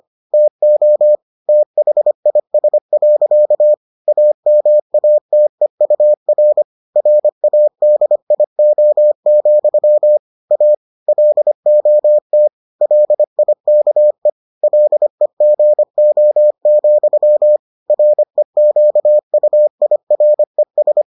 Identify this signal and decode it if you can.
Morse code, heard as a keyed tone, about 25 wpm, 600 Hz. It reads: TO THIS. AMATEUR RADIO, A LOT LIKE LEGO, REQUIRES